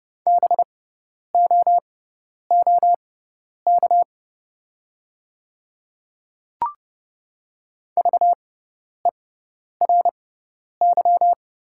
Morse code, heard as a keyed tone, about 30 wpm, spaced as for 10 wpm, 690 Hz. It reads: BOOK VERY
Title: 6x Character Spacing - Top 300 Words - 30wpm